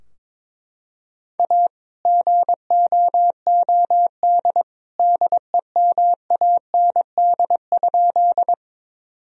Morse code, heard as a keyed tone, about 21 wpm, 700 Hz. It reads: A GOOD DEMAND?
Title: Sentences from Top 1000 Words - Review Part 4 - 22wpm